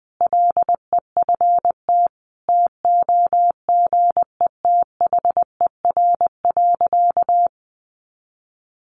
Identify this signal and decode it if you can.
Morse code, heard as a keyed tone, about 20 wpm, 700 Hz. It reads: LEFT TOGETHER.
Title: Sentences from Top 1100 Words 20wpm